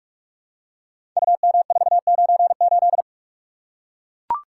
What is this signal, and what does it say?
Morse code, heard as a keyed tone, about 44 wpm, 700 Hz. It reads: UM498
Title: Code Groups - 5 Letter:Numbers - 45wpm